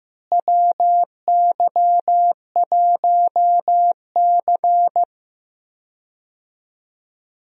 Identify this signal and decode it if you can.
Morse code, heard as a keyed tone, about 15 wpm, 700 Hz. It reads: WY1C
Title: USA Call Signs - 4 Characters - Repeated 3x - 15wpm